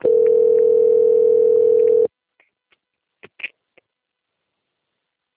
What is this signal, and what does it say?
Ringback tone (North America), 440 + 480 Hz.